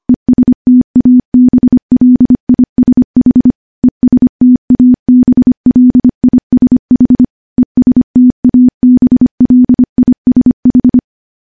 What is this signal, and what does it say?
Morse code, heard as a keyed tone, about 25 wpm, 250 Hz. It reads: ESTABLISH ESTABLISH ESTABLISH
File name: morse / morse-training-25WPM